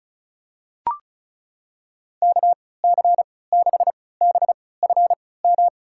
Morse code, heard as a keyed tone, about 36 wpm, spaced as for 20 wpm, 700 Hz. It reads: KC6BFM
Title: USA Call Signs - 6 Characters - 3x Character Spacing - 35wpm